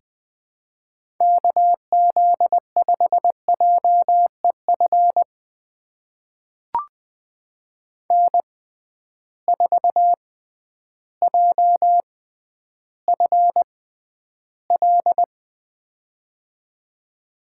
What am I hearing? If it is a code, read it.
Morse code, heard as a keyed tone, about 20 wpm, spaced as for 11 wpm, 700 Hz. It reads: KZ5JEF N4JFL
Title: USA Call Signs - 6x Character Spacing - 20wpm